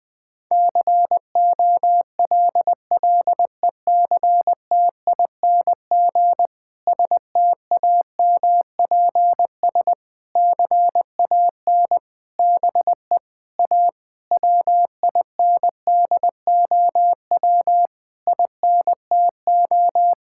Morse code, heard as a keyed tone, about 21 wpm, 700 Hz. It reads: COLLECTING STAMPS CAN BE A WINDOW INTO
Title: Difficult Sentences with Hints - 20wpm